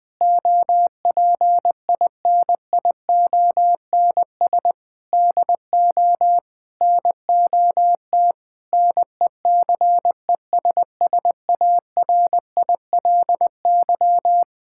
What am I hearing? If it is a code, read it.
Morse code, heard as a keyed tone, about 20 wpm, 700 Hz. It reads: OPINIONS DO NOT NECESSARILY
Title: Sentences with 8-Word Phrases - 20wpm